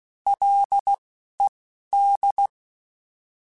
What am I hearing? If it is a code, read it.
Morse code, heard as a keyed tone, about 16 wpm, 800 Hz. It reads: L E D